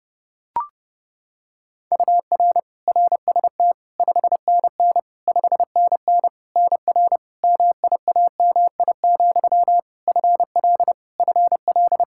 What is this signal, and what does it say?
Morse code, heard as a keyed tone, about 32 wpm, 700 Hz. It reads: UR RST 5NN 5NN NR MIAMI, FL FL
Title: K1USN SST QSO Practice 30wpm